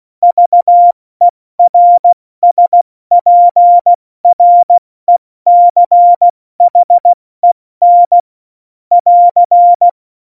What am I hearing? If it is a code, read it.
Morse code, heard as a keyed tone, about 16 wpm, 700 Hz. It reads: VERSPRECHEN <AR>
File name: Morsekurs / Witze / Witz 11. Juni 2019, 70 F80